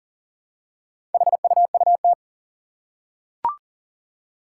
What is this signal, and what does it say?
Morse code, heard as a keyed tone, about 40 wpm, 700 Hz. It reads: HUUT
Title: Four Challenging Letter-Number Combinations 40wpm